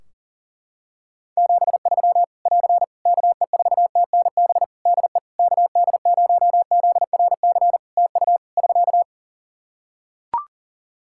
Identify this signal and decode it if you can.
Morse code, heard as a keyed tone, about 40 wpm, 700 Hz. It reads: 73 <AR> KE4TNB DE KD0ZRC TU <SK>
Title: K1USN SST QSO Practice 40wpm